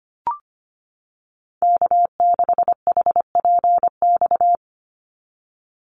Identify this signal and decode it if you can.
Morse code, heard as a keyed tone, about 25 wpm, 690 Hz. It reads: K6HPX
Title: NA SOTA Activator Callsigns 25wpm